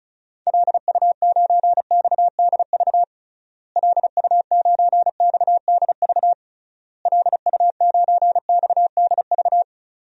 Morse code, heard as a keyed tone, about 34 wpm, 700 Hz. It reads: LU9XDV LU9XDV LU9XDV
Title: DXCC-100 Call Signs - 6 Characters - Repeated 3x - 35wpm